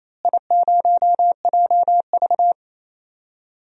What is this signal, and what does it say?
Morse code, heard as a keyed tone, about 28 wpm, 700 Hz. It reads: I0JV
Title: DXCC-100 Call Signs - 4 Characters - Repeated 2x - 28wpm